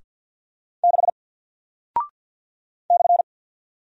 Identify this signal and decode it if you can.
Morse code, heard as a keyed tone, about 48 wpm, 700 Hz. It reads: 6 /
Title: Single Character (Letter:Number:Punctuation:Prosign) 50wpm